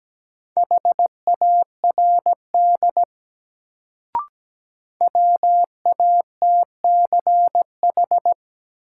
Morse code, heard as a keyed tone, about 17 wpm, 700 Hz. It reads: HARD WATCH
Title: Sentences from Top 300 Words - New Word Drill - 17wpm